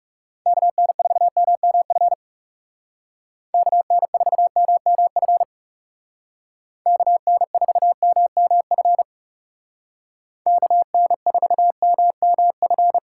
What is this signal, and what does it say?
Morse code, heard as a keyed tone, about 36 wpm, 700 Hz. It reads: KN4MMF KN4MMF KN4MMF KN4MMF
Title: Speed Racing - USA 2x3 Call Signs 30wpm